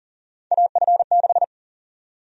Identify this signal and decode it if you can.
Morse code, heard as a keyed tone, about 40 wpm, 700 Hz. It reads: AF6
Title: USA Call Sign Prefixes - Repeated 2x - 40wpm